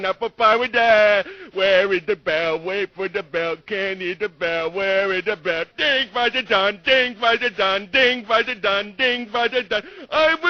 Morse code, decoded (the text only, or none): none